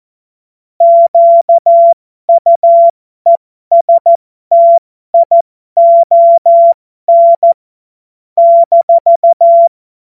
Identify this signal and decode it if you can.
Morse code, heard as a keyed tone, about 14 wpm, spaced as for 12 wpm, 680 Hz. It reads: QUESTION -
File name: Morsekurs / Witze